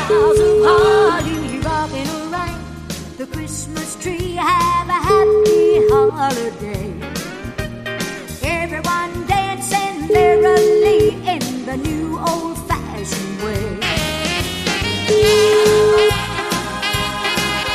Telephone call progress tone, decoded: ringback tone

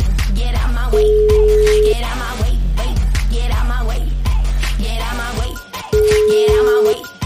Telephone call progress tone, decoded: ringback tone

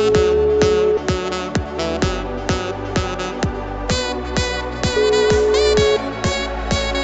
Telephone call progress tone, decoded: ringback tone

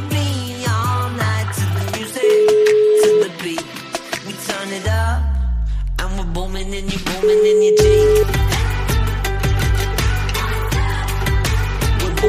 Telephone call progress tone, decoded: ringback tone

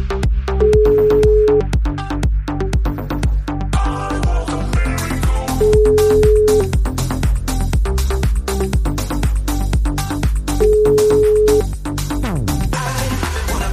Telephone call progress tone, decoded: ringback tone

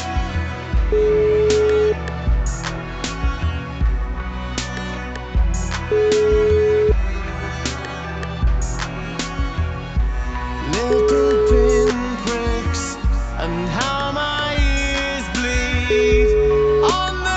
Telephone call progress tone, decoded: ringback tone

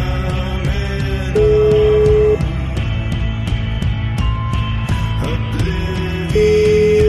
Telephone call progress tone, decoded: ringback tone

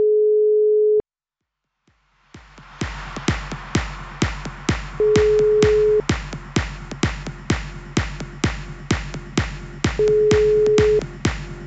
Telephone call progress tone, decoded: ringback tone

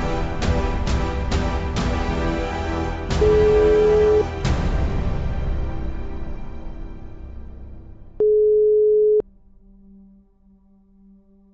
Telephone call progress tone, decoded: ringback tone

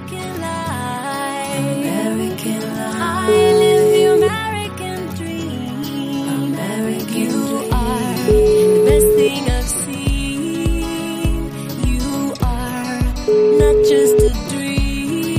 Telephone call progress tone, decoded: ringback tone

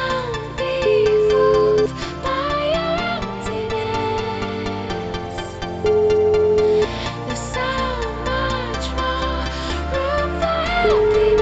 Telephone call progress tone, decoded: ringback tone

